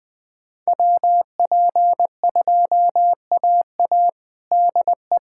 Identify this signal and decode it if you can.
Morse code, heard as a keyed tone, about 21 wpm, 700 Hz. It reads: WP2AA DE
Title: USA Callsign DE Callsign - 5 Characters - 20wpm